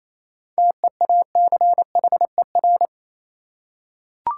Morse code, heard as a keyed tone, about 28 wpm, 700 Hz. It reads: TEACHER
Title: Seven Letter Words - 28wpm